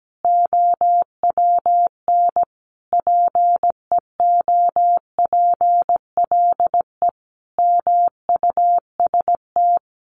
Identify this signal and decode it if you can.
Morse code, heard as a keyed tone, about 17 wpm, 700 Hz. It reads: OWN PEOPLE MUST
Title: Sentences from Top 300 Words 17wpm